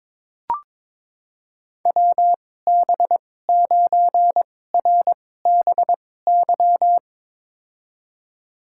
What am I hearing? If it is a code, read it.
Morse code, heard as a keyed tone, about 22 wpm, spaced as for 16 wpm, 700 Hz. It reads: WB9RBY
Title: USA Call Signs - 2x Character Spacing - 22wpm